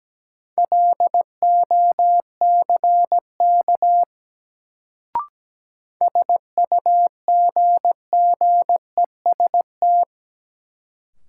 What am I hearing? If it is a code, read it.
Morse code, heard as a keyed tone, about 17 wpm, 700 Hz. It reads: LOCK SUGGEST
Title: Sentences from Top 1300 Words - New Word Drill Alt - 17wpm